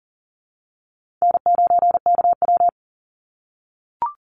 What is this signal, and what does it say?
Morse code, heard as a keyed tone, about 40 wpm, 700 Hz. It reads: N9KW
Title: NA SOTA Chaser Callsigns 40wpm